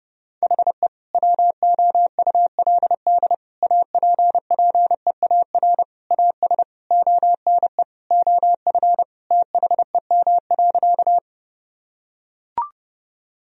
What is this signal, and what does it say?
Morse code, heard as a keyed tone, about 30 wpm, 700 Hz. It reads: HE WOULD APPEAR AS ONE OF THEM.
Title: Sentences from Top 1000 Words - Review Part 2 - 30wpm